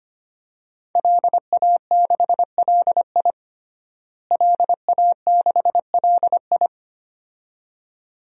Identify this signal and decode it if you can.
Morse code, heard as a keyed tone, about 25 wpm, 700 Hz. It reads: LA6LI LA6LI
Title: DXCC-100 Call Signs - 5 Characters - Repeated 2x - 25wpm